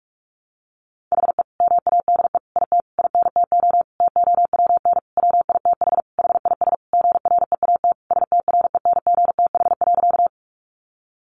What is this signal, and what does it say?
Morse code, heard as a keyed tone, about 44 wpm, 700 Hz. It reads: HE MADE IT INTO TOWN WITH HIS GREAT STRENGTH.
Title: Sentences from Top 1300 Words 45wpm